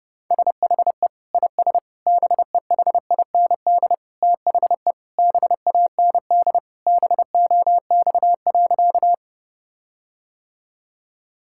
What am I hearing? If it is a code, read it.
Morse code, heard as a keyed tone, about 30 wpm, 700 Hz. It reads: SHE IS BEHIND THE BAND BOX.
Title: Sentences from Top 1600 Words 30wpm